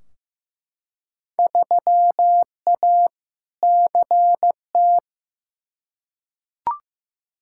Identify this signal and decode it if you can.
Morse code, heard as a keyed tone, about 15 wpm, 700 Hz. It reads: 3A CT
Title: Field Day Exchanges - 15wpm